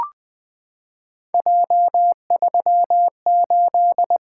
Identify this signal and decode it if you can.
Morse code, heard as a keyed tone, about 20 wpm, 700 Hz. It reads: J38
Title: Three Letter-Number Combinations - 20wpm